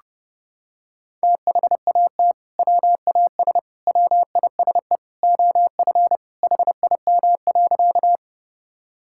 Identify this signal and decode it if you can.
Morse code, heard as a keyed tone, about 30 wpm, 700 Hz. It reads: THAT WAS WISE OF HIM.
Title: Sentences from Top 1600 Words 30wpm